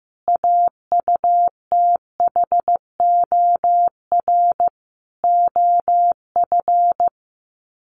Morse code, heard as a keyed tone, about 15 wpm, 700 Hz. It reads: AUTHOR OF
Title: Sets of 5 Words Encore2 15wpm